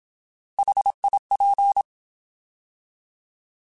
Morse code, heard as a keyed tone, about 27 wpm, 800 Hz. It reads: HIP
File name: Alphabetical / Alphabetical mp3s / Hs @25